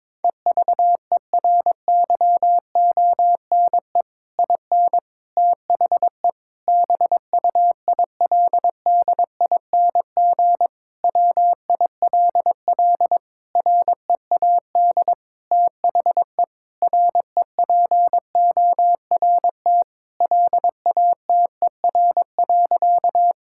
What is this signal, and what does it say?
Morse code, heard as a keyed tone, about 22 wpm, 700 Hz. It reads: EVERYONE IN THE BUILDING WILL READ THE REPORT LATER.
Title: Sentences from Top 1000 Words - Review Part 6 - 22wpm